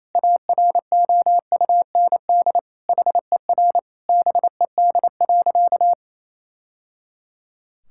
Morse code, heard as a keyed tone, about 27 wpm, 700 Hz. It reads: AROUND HER BED.
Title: Sentences from Top 1000 Words - Review Part 6 - 28wpm